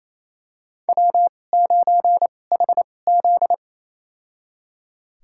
Morse code, heard as a keyed tone, about 28 wpm, spaced as for 20 wpm, 700 Hz. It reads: W9HZ
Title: USA Call Signs - 4 Characters - 2x Character Spacing - 28wpm